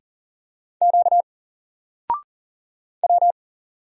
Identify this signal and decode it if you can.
Morse code, heard as a keyed tone, about 40 wpm, 700 Hz. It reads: Q W
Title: Single Letter-Number 40wpm